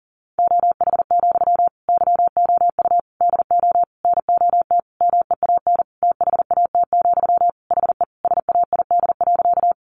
Morse code, heard as a keyed tone, about 40 wpm, 700 Hz. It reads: OH, YOU DO NOT MEAN THAT, HE SAID.
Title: Sentences from Top 600 Words 40wpm